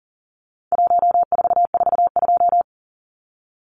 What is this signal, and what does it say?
Morse code, heard as a keyed tone, about 40 wpm, 700 Hz. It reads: 1442
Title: Numbers - 4-Digit 40wpm